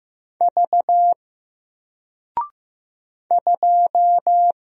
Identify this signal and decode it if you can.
Morse code, heard as a keyed tone, about 15 wpm, 700 Hz. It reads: V 2
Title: Single Letter-Number Alt 15wpm